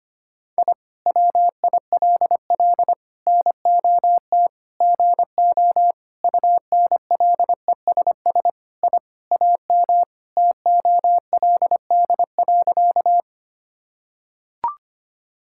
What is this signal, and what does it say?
Morse code, heard as a keyed tone, about 25 wpm, 700 Hz. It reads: I WILL NOT GO UNLESS I AM TOLD.
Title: Sentences from Top 1000 Words - Review Part 6 - 25wpm